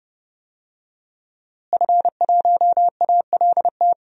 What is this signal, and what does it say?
Morse code, heard as a keyed tone, about 32 wpm, 700 Hz. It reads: F1ALT
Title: DXCC-100 Call Signs - 5 Characters - Repeated 3x - 30wpm